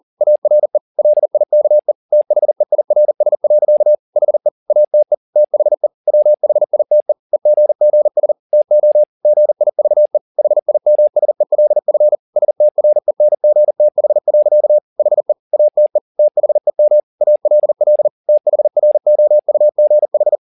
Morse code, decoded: ARE LIKE THEIRS. HE ATE THE WHITE EGGS TO GIVE HIMSELF STRENGTH. HE ATE THEM ALL THROUGH

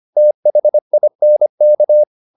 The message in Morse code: THINK

